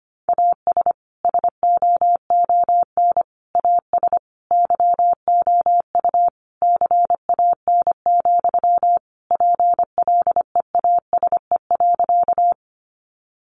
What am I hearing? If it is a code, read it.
Morse code, heard as a keyed tone, about 25 wpm, 700 Hz. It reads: AS SOON AS YOU CAN, PLEASE.
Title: Sentences from Top 1500 Words 25wpm